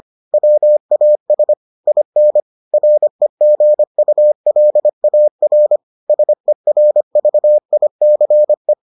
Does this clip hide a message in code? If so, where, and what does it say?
Yes: Morse code from 0.3 to 8.7 s, heard as a keyed tone, about 25 wpm, 600 Hz. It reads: WAS IN REGULAR SERVICE